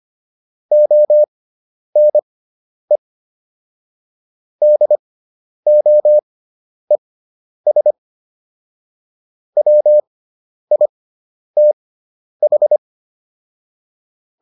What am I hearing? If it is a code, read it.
Morse code, heard as a keyed tone, about 25 wpm, spaced as for 10 wpm, 600 Hz. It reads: ONE DOES WITH